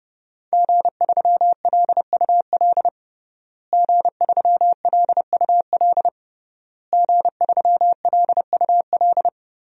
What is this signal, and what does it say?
Morse code, heard as a keyed tone, about 32 wpm, 700 Hz. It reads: G3LUL G3LUL G3LUL
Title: DXCC-100 Call Signs - 5 Characters - Repeated 3x - 30wpm